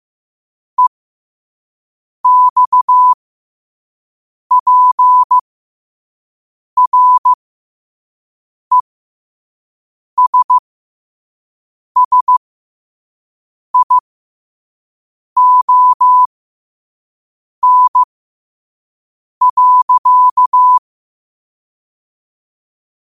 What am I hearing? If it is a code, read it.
Morse code, heard as a keyed tone, about 15 wpm, spaced as for 5 wpm, 1000 Hz. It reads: EXPRESSION.
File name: qotdfiles / 5wpm